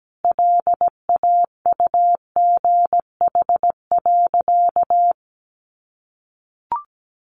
Morse code, heard as a keyed tone, about 17 wpm, 700 Hz. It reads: LAUGH.
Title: Sentences from Top 1500 Words 17wpm